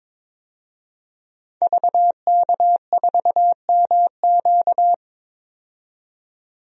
Morse code, heard as a keyed tone, about 22 wpm, 700 Hz. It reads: VK4MQ
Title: DXCC-100 Call Signs - 5 Characters - 22wpm